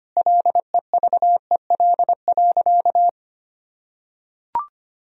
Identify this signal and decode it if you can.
Morse code, heard as a keyed tone, about 25 wpm, 700 Hz. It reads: LEVEL.
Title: Sentences from Top 1000 Words - Review Part 4 - 25wpm